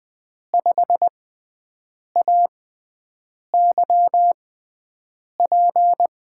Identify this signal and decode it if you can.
Morse code, heard as a keyed tone, about 20 wpm, spaced as for 7 wpm, 700 Hz. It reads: 5AYP
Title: USA Call Signs - 6 Characters - 6x Character Spacing - 20wpm